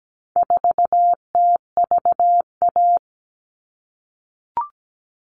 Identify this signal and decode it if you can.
Morse code, heard as a keyed tone, about 17 wpm, 700 Hz. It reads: 4TVA
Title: All Cut Numbers - 4-Digit 17wpm